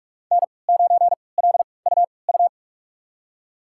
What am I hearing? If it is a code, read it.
Morse code, heard as a keyed tone, about 41 wpm, spaced as for 25 wpm, 700 Hz. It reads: N9LUU
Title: USA Call Signs - 5 Characters - 3x Character Spacing - 45wpm